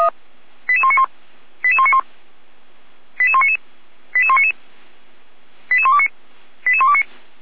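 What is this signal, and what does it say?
Five-tone selective call (ZVEI), heard as quick runs of five tones, tones of about 70 ms. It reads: 88191 88191 88199 88199 88129 88129
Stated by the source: ZVEI five-tone call received by SDR.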